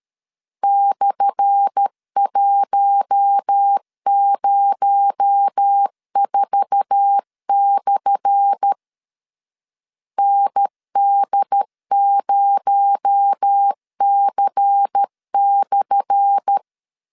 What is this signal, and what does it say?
Morse code, heard as a keyed tone, about 13 wpm, 800 Hz. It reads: /104/ ND0C/
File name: LEZIONI / 10 SIGNS / 10A SLASH 12wpm 2314